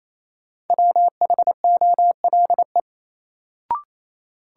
Morse code, heard as a keyed tone, about 28 wpm, 700 Hz. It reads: WHOLE
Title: Sets of 2 Words - New Word Drill - 28wpm